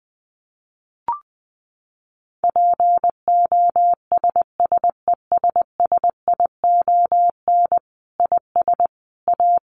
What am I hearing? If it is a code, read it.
Morse code, heard as a keyed tone, about 20 wpm, 700 Hz. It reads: POSSESSION IS A
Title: Sentences from Top 1400 Words 20wpm